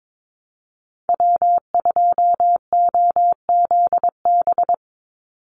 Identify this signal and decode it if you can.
Morse code, heard as a keyed tone, about 22 wpm, 700 Hz. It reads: W2OZB
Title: Call Signs Repeated 3x - 22wpm